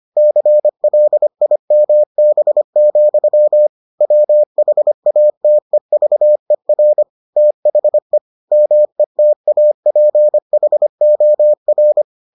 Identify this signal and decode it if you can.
Morse code, heard as a keyed tone, about 25 wpm, 600 Hz. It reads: CLIMB, WHATEVER THE METAPHOR